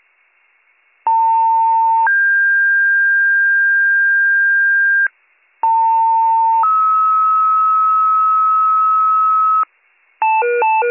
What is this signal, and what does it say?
Two-tone page, heard as a one-second tone, then a three-second tone, about 900 then 1600 Hz, about 900 then 1290 Hz, with noise. It is heard from 1.1 to 9.6 s.